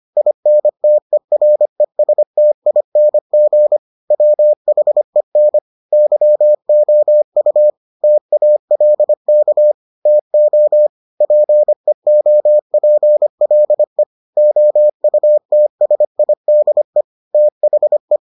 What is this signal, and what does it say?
Morse code, heard as a keyed tone, about 25 wpm, 600 Hz. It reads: INTERESTING WHEN YOU TALK TO PEOPLE OUTSIDE THE